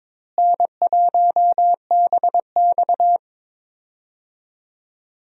Morse code, heard as a keyed tone, about 22 wpm, 700 Hz. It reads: N1BX
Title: USA Call Signs - 4 Characters - Repeated 3x - 22wpm